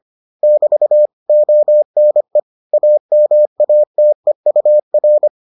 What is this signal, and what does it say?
Morse code, heard as a keyed tone, about 25 wpm, 600 Hz. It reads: <BT> ONE AMATEUR